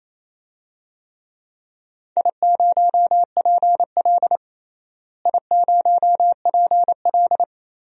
Morse code, heard as a keyed tone, about 28 wpm, 700 Hz. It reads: I0PL I0PL
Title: DXCC-100 Call Signs - 4 Characters - Repeated 2x - 28wpm